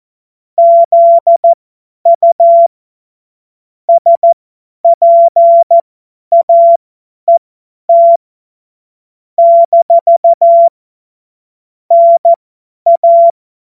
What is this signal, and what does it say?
Morse code, heard as a keyed tone, about 14 wpm, spaced as for 10 wpm, 680 Hz. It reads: ZU SPAET - NA
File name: Morsekurs / Witze / Witz 20. Januar 2026 50 F70